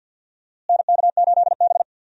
Morse code, heard as a keyed tone, about 50 wpm, 700 Hz. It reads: NK8B
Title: USA Call Signs - 4 Characters - Repeated 2x - 50wpm